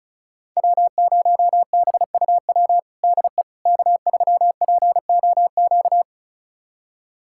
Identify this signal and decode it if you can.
Morse code, heard as a keyed tone, about 34 wpm, 700 Hz. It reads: W0BUW DE K3POQ
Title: USA Callsign DE Callsign - 5 Characters - 35wpm